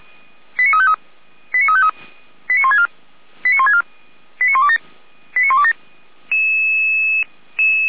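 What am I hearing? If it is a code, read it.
Five-tone selective call (ZVEI), heard as quick runs of five tones, tones of about 80 ms. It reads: 89353 89353 89164 89164 89127 89127
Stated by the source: Five-tone ZVEI selective call 89508, received by SDR.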